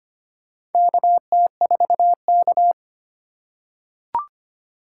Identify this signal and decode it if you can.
Morse code, heard as a keyed tone, about 25 wpm, 700 Hz. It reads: KT4K
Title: USA Call Signs - 4 Characters - 2x Character Spacing - 25wpm